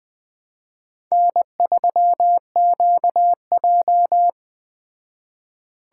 Morse code, heard as a keyed tone, about 20 wpm, 700 Hz. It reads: N3QJ
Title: Speed Racing - USA 1x2 Call Signs 17wpm